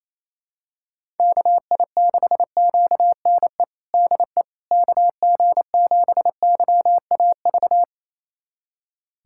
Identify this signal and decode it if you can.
Morse code, heard as a keyed tone, about 28 wpm, 700 Hz. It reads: KI6QNE DE KG7YAV
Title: USA Callsign DE Callsign 28wpm